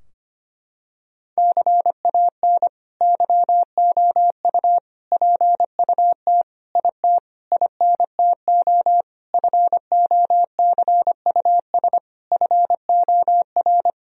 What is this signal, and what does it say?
Morse code, heard as a keyed tone, about 25 wpm, 700 Hz. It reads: CAN YOU PUT IT INTO FOCUS FOR
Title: Sentences from Top 1000 Words - Review Part 4 - 25wpm